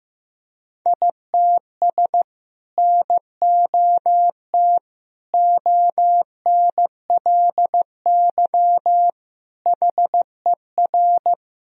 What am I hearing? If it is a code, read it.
Morse code, heard as a keyed tone, about 15 wpm, 700 Hz. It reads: ITS NOT ONLY HER
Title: Sentences from Top 1000 Words - Review Part 2 - 15wpm